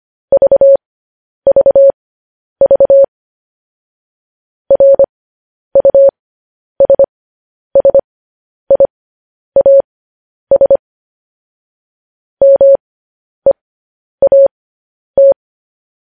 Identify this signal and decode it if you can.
Morse code, heard as a keyed tone, about 25 wpm, spaced as for 10 wpm, 550 Hz. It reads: VVV RUSSIAS MEAT
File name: podcast / News headlines-1x-25x10